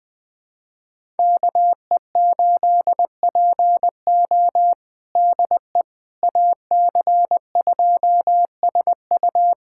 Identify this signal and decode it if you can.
Morse code, heard as a keyed tone, about 20 wpm, 700 Hz. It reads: KE8PO DE AC2SU
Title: USA Callsign DE Callsign - 5 Characters - 20wpm